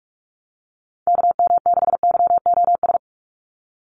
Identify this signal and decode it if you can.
Morse code, heard as a keyed tone, about 44 wpm, 700 Hz. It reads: KM6YOS
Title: USA Call Signs 45wpm